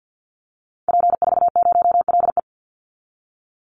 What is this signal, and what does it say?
Morse code, heard as a keyed tone, about 50 wpm, 700 Hz. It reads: P40LE
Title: Call Signs - 50wpm